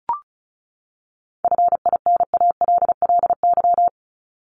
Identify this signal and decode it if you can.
Morse code, heard as a keyed tone, about 34 wpm, 700 Hz. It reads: FINALLY
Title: LY Suffix - Top 100 Words with Base Words 35wpm